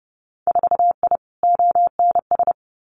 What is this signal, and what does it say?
Morse code, heard as a keyed tone, about 32 wpm, 700 Hz. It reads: 4I ONS
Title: Winter Field Day Exchanges Encore 30wpm